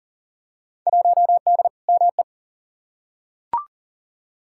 Morse code, heard as a keyed tone, about 40 wpm, 700 Hz. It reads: 1D ME
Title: Field Day Exchanges - 40wpm